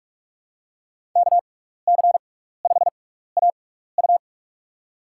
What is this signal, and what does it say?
Morse code, heard as a keyed tone, about 46 wpm, spaced as for 15 wpm, 700 Hz. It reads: KC5AU
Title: USA Call Signs - 5 Characters - 6x Character Spacing - 45wpm